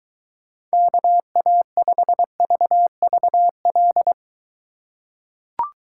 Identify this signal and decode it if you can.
Morse code, heard as a keyed tone, about 22 wpm, 700 Hz. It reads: KA5VVL